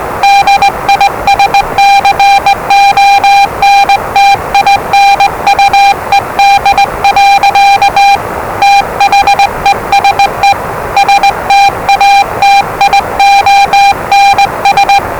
Morse code, heard as a keyed tone, about 20 wpm, 800 Hz, with noise. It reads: DISCONTINUED. THESE STATIONS
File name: WellingtonRadioZlwCloseDownOnHfAndMf / 01-Zlw 4277khz 417.5 qrt until 0925z